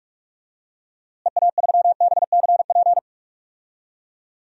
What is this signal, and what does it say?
Morse code, heard as a keyed tone, about 46 wpm, 700 Hz. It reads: EA3BCP
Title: DXCC-100 Call Signs - 6 Characters - Repeated 3x - 45wpm